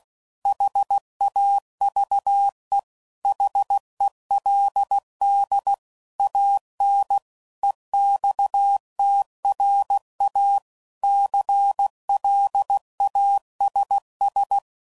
Morse code, heard as a keyed tone, about 16 wpm, 800 Hz. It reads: HAVE HELD AN EXTRA CLASS
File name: CW Code Groups @15WPM / QSOs / kc7haa